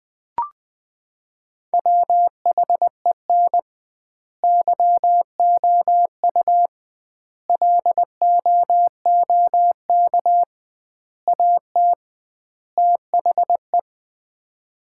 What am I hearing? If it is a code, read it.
Morse code, heard as a keyed tone, about 20 wpm, 700 Hz. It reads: WHEN YOU LOOK AT THE
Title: 2x Word-Spacing - Sets of 5 Words - 20wpm